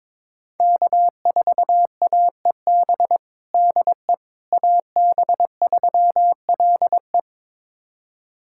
Morse code, heard as a keyed tone, about 22 wpm, 700 Hz. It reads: K4AEB DE AB3LE
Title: USA Callsign DE Callsign - 5 Characters - 22wpm